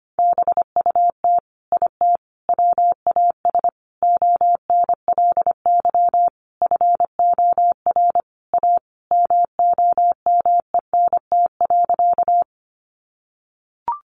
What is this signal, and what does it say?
Morse code, heard as a keyed tone, about 25 wpm, 700 Hz. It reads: BUT IT WAS ONLY FOR A MOMENT.
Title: Sentences from Top 1000 Words 25wpm